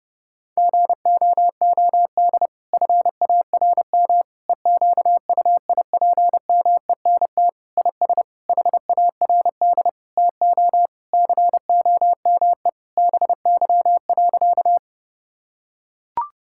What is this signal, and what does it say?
Morse code, heard as a keyed tone, about 32 wpm, 700 Hz. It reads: GOOD FARM EQUIPMENT IS HARD TO COME BY.
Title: Sentences from Top 1000 Words - Review Part 5 - 30wpm